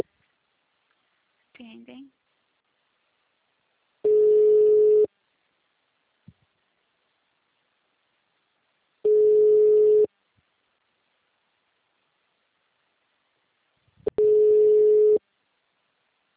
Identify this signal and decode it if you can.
Ringback tone (Europe, 425 Hz), with speech.